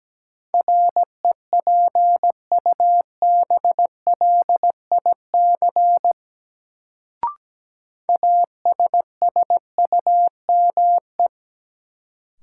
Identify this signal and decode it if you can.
Morse code, heard as a keyed tone, about 17 wpm, 700 Hz. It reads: REPUBLIC ASSUME
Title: Sentences from Top 1500 Words - New Word Drill Alt - 17wpm